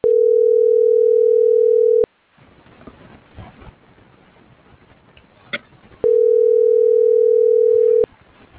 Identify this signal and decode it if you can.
Ringback tone (North America), 440 + 480 Hz.